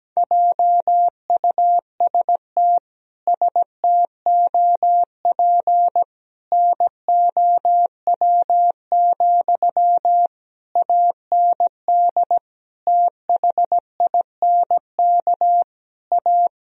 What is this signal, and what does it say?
Morse code, heard as a keyed tone, about 17 wpm, 700 Hz. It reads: JUST STOP NOW, AND THINK A
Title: Sentences from Top 1000 Words - Review Part 1 - 17wpm